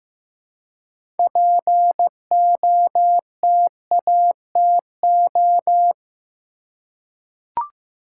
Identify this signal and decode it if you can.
Morse code, heard as a keyed tone, about 15 wpm, 700 Hz. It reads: POTATO
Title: Six Letter Words - 15wpm